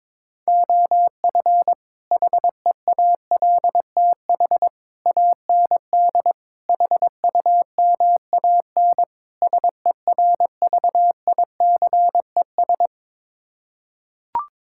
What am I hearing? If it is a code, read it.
Morse code, heard as a keyed tone, about 22 wpm, 700 Hz. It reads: OF HEALTH AND HUMAN SERVICES
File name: practice / 22wpm / Build A Phrase - 5-Word Phrases - 22wpm